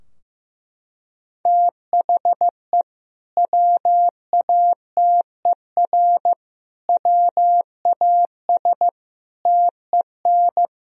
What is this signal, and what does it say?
Morse code, heard as a keyed tone, about 15 wpm, 700 Hz. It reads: THE WATER WAS TEN
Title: Sentences from Top 1000 Words - Review Part 3 - 15wpm